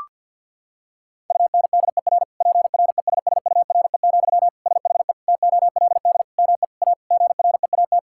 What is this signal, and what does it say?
Morse code, heard as a keyed tone, about 50 wpm, 700 Hz. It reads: UNDER PRESSURE, SHE TOLD ME A GREAT
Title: Sentences from Top 1000 Words - Review Part 6 - 50wpm